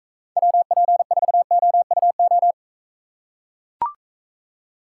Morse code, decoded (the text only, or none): WP4OUO